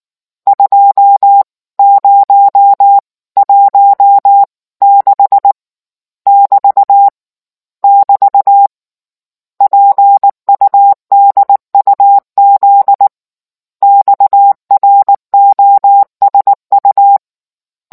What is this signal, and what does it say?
Morse code, heard as a keyed tone, about 19 wpm, 800 Hz. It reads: 2016 <BT> <BT> PUDUZ XROSU